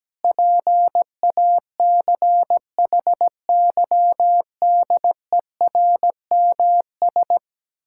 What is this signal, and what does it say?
Morse code, heard as a keyed tone, about 17 wpm, 700 Hz. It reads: PACHYDERMS
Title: Pangram Challenge v2 - 17wpm